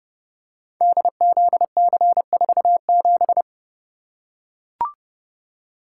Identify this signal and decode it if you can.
Morse code, heard as a keyed tone, about 30 wpm, 700 Hz. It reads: DZC47
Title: Code Groups - 5 Letter:Numbers - 30wpm